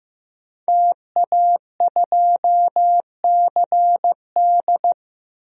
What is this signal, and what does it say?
Morse code, heard as a keyed tone, about 15 wpm, 700 Hz. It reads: TA2CD
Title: DXCC-100 Call Signs - 5 Characters - Repeated 2x - 15wpm